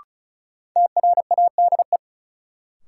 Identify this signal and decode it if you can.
Morse code, heard as a keyed tone, about 34 wpm, 700 Hz. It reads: TRADE
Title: Sets of 5 Words Encore - New Word Drill - 35wpm